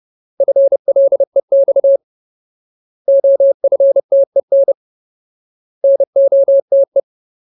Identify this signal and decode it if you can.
Morse code, heard as a keyed tone, about 30 wpm, 550 Hz. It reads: FLEX OFTEN NOTE
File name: Ultimate Word List / 30WPM Ultimate